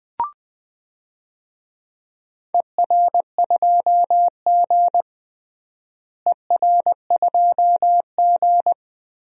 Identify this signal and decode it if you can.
Morse code, heard as a keyed tone, about 21 wpm, 700 Hz. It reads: ER2G ER2G
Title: DXCC-100 Call Signs - 4 Characters - Repeated 2x - 20wpm